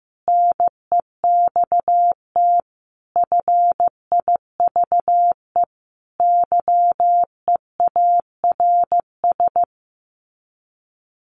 Morse code, heard as a keyed tone, about 15 wpm, 700 Hz. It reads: NEXT FIVE YEARS
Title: Sets of 5 Words 15wpm